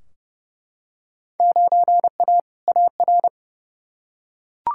Morse code, heard as a keyed tone, about 30 wpm, 700 Hz. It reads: 9A AR
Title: Field Day Exchanges Encore - 30wpm